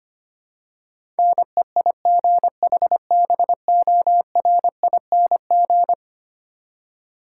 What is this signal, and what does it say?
Morse code, heard as a keyed tone, about 25 wpm, 700 Hz. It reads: NEIGHBORING
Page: morse-code-ninja-practice-25wpm